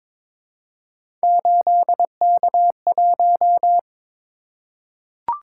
Morse code, heard as a keyed tone, about 22 wpm, 700 Hz. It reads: 8K1
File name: practice / 22wpm / Three Letter-Number Combinations - 22wpm